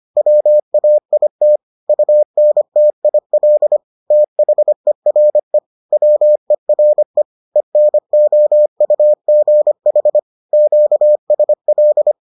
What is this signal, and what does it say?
Morse code, heard as a keyed tone, about 25 wpm, 600 Hz. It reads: WAIT UNTIL THERE WERE ENOUGH QSL